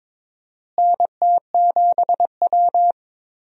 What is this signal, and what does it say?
Morse code, heard as a keyed tone, about 22 wpm, 700 Hz. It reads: NT7W
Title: USA Call Signs - 4 Characters - Repeated 2x - 22wpm